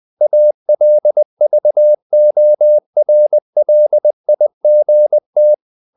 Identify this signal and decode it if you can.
Morse code, heard as a keyed tone, about 20 wpm, 600 Hz. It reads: ALVORLIGT
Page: Ferie 20 wpm